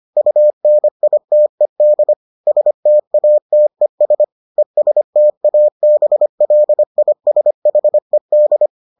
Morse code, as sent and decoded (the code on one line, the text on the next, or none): ..- -. .. - . -.. / ... - .- - . ... / . ... - .- -... .-.. .. ... .... -..
UNITED STATES ESTABLISHED